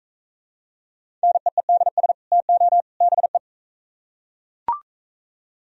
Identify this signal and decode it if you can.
Morse code, heard as a keyed tone, about 39 wpm, 700 Hz. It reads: NEEDS TO BE